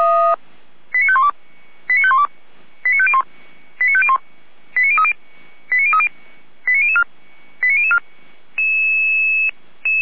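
Five-tone selective call (ZVEI), heard as quick runs of five tones, tones of about 70 ms. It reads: 89512 89512 89501 89501 89030 89030 89004 89004
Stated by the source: ZVEI five-tone call received by SDR, recorded by feuerwehr-orlamuende (fire brigade Orlamünde).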